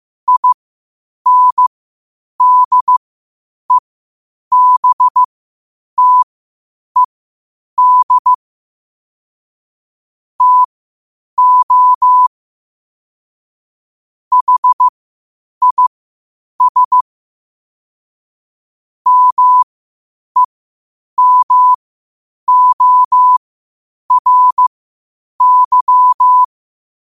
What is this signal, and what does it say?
Morse code, heard as a keyed tone, about 15 wpm, spaced as for 8 wpm, 1000 Hz. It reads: INDEBTED TO HIS MEMORY
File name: qotdfiles / 8wpm